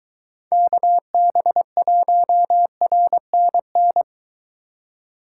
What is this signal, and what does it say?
Morse code, heard as a keyed tone, about 23 wpm, 700 Hz. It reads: KB1RNN